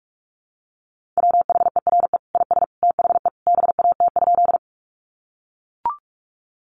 Morse code, heard as a keyed tone, about 44 wpm, 700 Hz. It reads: WHERE IS THE BAT?
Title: Sentences from Top 1500 Words 45wpm